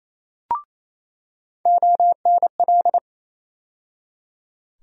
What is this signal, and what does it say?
Morse code, heard as a keyed tone, about 28 wpm, 700 Hz. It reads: ONL
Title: USA Call Sign Postfixes - 3 Letters 28wpm